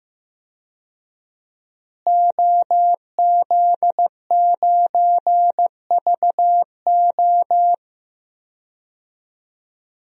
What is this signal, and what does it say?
Morse code, heard as a keyed tone, about 15 wpm, 700 Hz. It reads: OZ9VO
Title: DXCC-100 Call Signs - 5 Characters - Repeated 2x - 15wpm